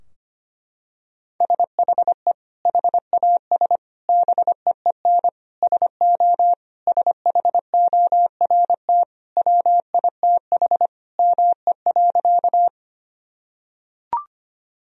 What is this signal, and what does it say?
Morse code, heard as a keyed tone, about 25 wpm, 700 Hz. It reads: SHE HAS BEEN SO SHORT WITH ME.
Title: Sentences from Top 1000 Words - Review Part 2 - 25wpm